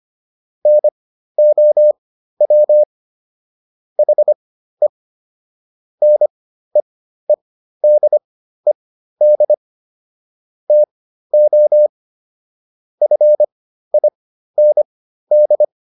Morse code, heard as a keyed tone, about 25 wpm, spaced as for 13 wpm, 600 Hz. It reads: NOW HE NEEDED TO FIND
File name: SS109 13